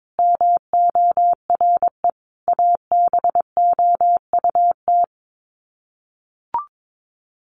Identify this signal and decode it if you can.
Morse code, heard as a keyed tone, about 22 wpm, 700 Hz. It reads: MORE ABOUT
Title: Sets of 5 Words Encore2 22wpm